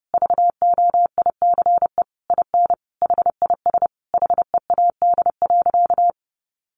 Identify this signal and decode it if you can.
Morse code, heard as a keyed tone, about 28 wpm, 700 Hz. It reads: VOICE IN HIS HEAD.
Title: Sentences from Top 1300 Words 30wpm